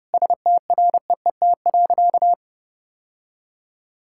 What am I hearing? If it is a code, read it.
Morse code, heard as a keyed tone, about 29 wpm, 700 Hz. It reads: STREET.
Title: Sentences from Top 1000 Words - Review Part 6 - 30wpm